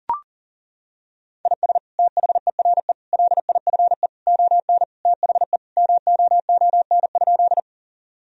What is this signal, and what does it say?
Morse code, heard as a keyed tone, about 37 wpm, 700 Hz. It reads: IS THERE LIFE ON THE MOON?
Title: Sentences from Top 1000 Words - Review Part 3 - 40wpm